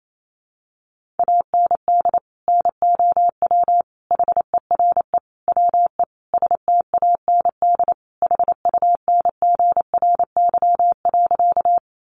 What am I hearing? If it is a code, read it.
Morse code, heard as a keyed tone, about 28 wpm, 700 Hz. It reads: AND NOW HERE WE STAND HUNGRY.
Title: Sentences from Top 1400 Words 28wpm